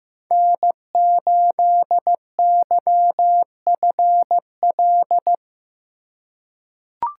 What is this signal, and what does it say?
Morse code, heard as a keyed tone, about 15 wpm, 700 Hz. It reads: N8YFL
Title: USA Call Signs - 5 Characters - 2x Character Spacing - 15wpm